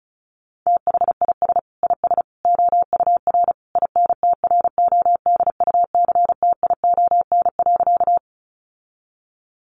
Morse code, heard as a keyed tone, about 34 wpm, 700 Hz. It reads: THIS IS OUR INTRODUCTION.
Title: Sentences from Top 1000 Words 35wpm